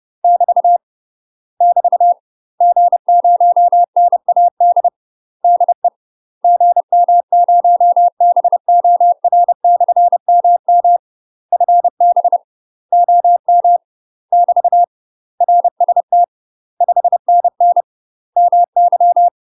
Morse code, heard as a keyed tone, about 30 wpm, 700 Hz. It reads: <BT> <BT> G0NAD DE GM0BOR/MM FB OM <BT> RST 5NN MY
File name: UK MASTER QSOs TRAINING / 30WPM / UK MASTER QSOs 0043